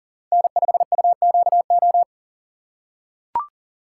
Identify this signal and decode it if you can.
Morse code, heard as a keyed tone, about 38 wpm, 700 Hz. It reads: N5UQO
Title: USA Call Signs - 5 Characters - 3x Character Spacing - 40wpm